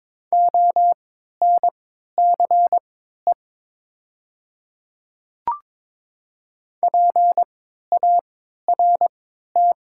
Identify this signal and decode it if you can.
Morse code, heard as a keyed tone, about 22 wpm, spaced as for 12 wpm, 700 Hz. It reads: ONCE PART
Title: 3x Character Spacing - Top 300 Words - 22wpm